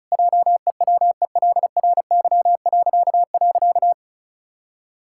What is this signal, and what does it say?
Morse code, heard as a keyed tone, about 34 wpm, 700 Hz. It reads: JEWELRY..
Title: Pangram Challenge v2 - 35wpm